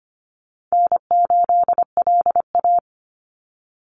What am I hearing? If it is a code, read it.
Morse code, heard as a keyed tone, about 25 wpm, 700 Hz. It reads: N8LA
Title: Easy Call Signs - 25wpm